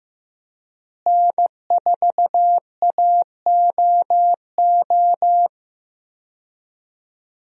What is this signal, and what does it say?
Morse code, heard as a keyed tone, about 15 wpm, 700 Hz. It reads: N4AOO
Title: USA Call Signs - 5 Characters - Repeated 2x - 15wpm